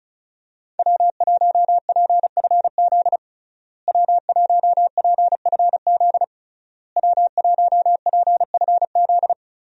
Morse code, heard as a keyed tone, about 36 wpm, 700 Hz. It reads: W1PFZ W1PFZ W1PFZ
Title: USA Call Signs - 5 Characters - Repeated 3x - 35wpm